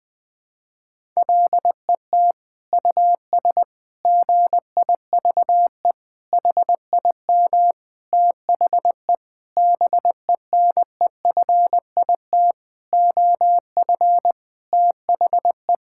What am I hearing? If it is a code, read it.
Morse code, heard as a keyed tone, about 20 wpm, 700 Hz. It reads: LET US GIVE HIM THE BENEFIT OF THE
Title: Sentences with 7-Word Phrases - Spoken Beginning - 20wpm